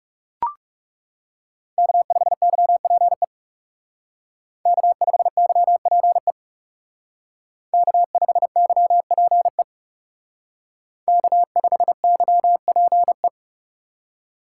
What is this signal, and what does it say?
Morse code, heard as a keyed tone, about 36 wpm, 700 Hz. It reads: K5YPE K5YPE K5YPE K5YPE
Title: Speed Racing - USA 1x3 Call Signs 30wpm